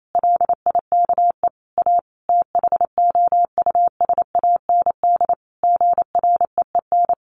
Morse code, heard as a keyed tone, about 28 wpm, 700 Hz. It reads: LIKE A THOUSAND GREEN
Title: A Sound Of Thunder 28wpm